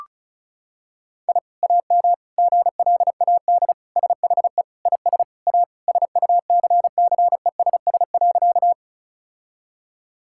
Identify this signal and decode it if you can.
Morse code, heard as a keyed tone, about 36 wpm, 700 Hz. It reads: I AM GLAD SHE IS A SUCCESS.
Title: Sentences from Top 1000 Words - Review Part 6 - 35wpm